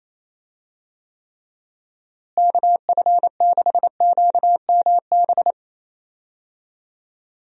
Morse code, heard as a keyed tone, about 27 wpm, 700 Hz. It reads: KF6QMB
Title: USA Call Signs - 6 Characters - Repeated 3x - 28wpm